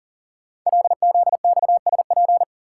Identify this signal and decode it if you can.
Morse code, heard as a keyed tone, about 42 wpm, 700 Hz. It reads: LZXSP
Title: Code Groups - 5 Letters - 40wpm